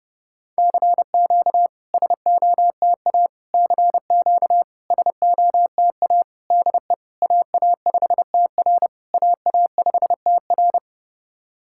Morse code, heard as a keyed tone, about 30 wpm, 690 Hz. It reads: CQ SOTA CQ SOTA DE AA5TR AA5TR
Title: SOTA QSO Practice 30wpm